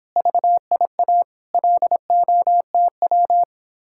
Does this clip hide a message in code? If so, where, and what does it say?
Yes: Morse code from 0.2 to 3.4 s, heard as a keyed tone, about 26 wpm, 700 Hz. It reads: VIA LOTW